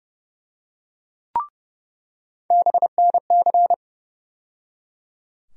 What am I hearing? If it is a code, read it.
Morse code, heard as a keyed tone, about 30 wpm, 700 Hz. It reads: BNC
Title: USA Call Sign Postfixes - 3 Letters 30wpm